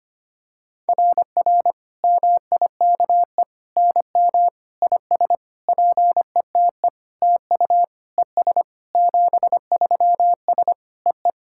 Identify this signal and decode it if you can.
Morse code, heard as a keyed tone, about 25 wpm, 700 Hz. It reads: RR MIKE NM IS PETE TU ES 73S EE
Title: SOTA QSO Practice 25wpm